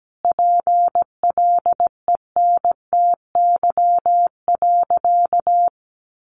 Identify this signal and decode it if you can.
Morse code, heard as a keyed tone, about 17 wpm, 700 Hz. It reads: PLENTY.
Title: Sentences from Top 1300 Words 17wpm